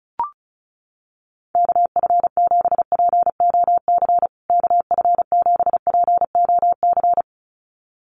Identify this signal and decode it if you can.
Morse code, heard as a keyed tone, about 34 wpm, 700 Hz. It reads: KF7POC KF7POC
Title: USA Call Signs Repeated 2x 35wpm